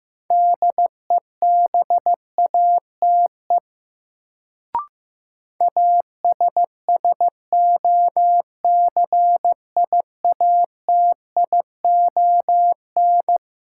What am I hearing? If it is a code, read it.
Morse code, heard as a keyed tone, about 15 wpm, 700 Hz. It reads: DEBATE ASSOCIATION
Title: Sentences from Top 1100 Words - New Word Drill - 15wpm